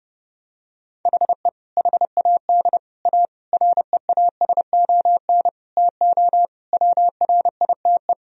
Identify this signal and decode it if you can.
Morse code, heard as a keyed tone, about 30 wpm, 700 Hz. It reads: HE HAD A REASON TO WRITE
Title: Sentences from Top 1000 Words - Review Part 1 - 30wpm